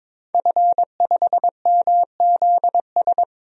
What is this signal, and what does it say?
Morse code, heard as a keyed tone, about 22 wpm, 700 Hz. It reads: F5MZS
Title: DXCC-100 Call Signs - 5 Characters - Repeated 3x - 22wpm